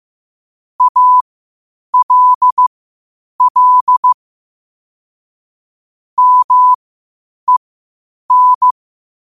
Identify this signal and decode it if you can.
Morse code, heard as a keyed tone, about 15 wpm, spaced as for 8 wpm, 1000 Hz. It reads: ALL MEN